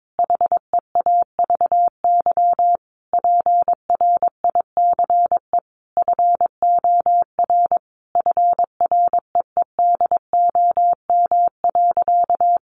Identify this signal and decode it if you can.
Morse code, heard as a keyed tone, about 22 wpm, 700 Hz. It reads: HEAVY PRICE FOR FREEDOM.
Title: Sentences from Top 1100 Words 22wpm